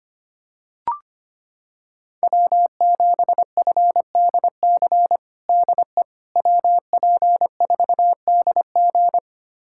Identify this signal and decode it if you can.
Morse code, heard as a keyed tone, about 25 wpm, 700 Hz. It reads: W7FDC DE WP4DG
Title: USA Callsign DE Callsign - 5 Characters - 25wpm